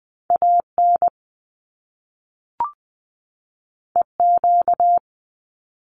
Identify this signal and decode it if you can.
Morse code, heard as a keyed tone, about 20 wpm, 700 Hz. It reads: AN EQ
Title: Top Two Letter Combinations in Words 20wpm